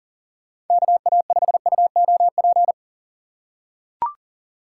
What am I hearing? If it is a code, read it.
Morse code, heard as a keyed tone, about 40 wpm, 700 Hz. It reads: KA5UOP
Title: USA Call Signs - 6 Characters - 2x Character Spacing - 40wpm